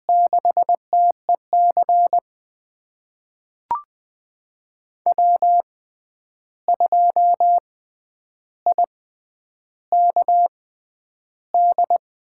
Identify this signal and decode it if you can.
Morse code, heard as a keyed tone, about 20 wpm, spaced as for 10 wpm, 700 Hz. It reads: 6TEC W2IKD
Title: USA Call Signs - 5 Characters - 6x Character Spacing - 20wpm